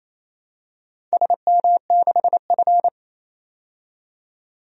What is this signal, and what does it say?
Morse code, heard as a keyed tone, about 28 wpm, 700 Hz. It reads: SM6F
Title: DXCC-100 Call Signs - 4 Characters - 28wpm